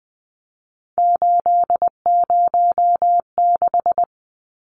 Morse code, heard as a keyed tone, about 20 wpm, 700 Hz. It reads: 806
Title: Numbers - 3-Digit 20wpm